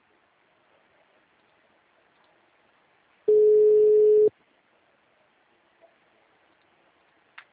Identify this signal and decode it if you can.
Ringback tone (Europe, 425 Hz).